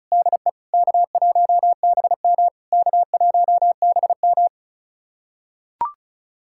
Morse code, heard as a keyed tone, about 34 wpm, 690 Hz. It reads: DE K1BM K1BM
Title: SOTA QSO Practice 35wpm